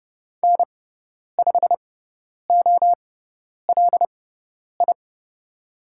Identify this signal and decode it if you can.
Morse code, heard as a keyed tone, about 30 wpm, spaced as for 10 wpm, 700 Hz. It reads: N5OLI
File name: Lesson 17g - Letter u - Call Signs 30wpm